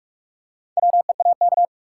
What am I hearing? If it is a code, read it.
Morse code, heard as a keyed tone, about 46 wpm, 700 Hz. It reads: WEAK
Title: Four Letter Words - 45wpm